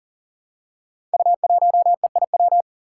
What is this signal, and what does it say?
Morse code, heard as a keyed tone, about 38 wpm, 700 Hz. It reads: U1EIW
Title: Five Challenging Letter-Number Combinations 40wpm